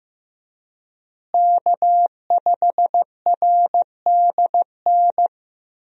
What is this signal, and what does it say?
Morse code, heard as a keyed tone, about 15 wpm, 700 Hz. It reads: K5RDN
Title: USA Call Signs - 5 Characters - Repeated 2x - 15wpm